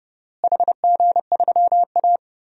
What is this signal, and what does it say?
Morse code, heard as a keyed tone, about 31 wpm, 700 Hz. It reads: HG3A
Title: DXCC-100 Call Signs - 4 Characters - Repeated 2x - 30wpm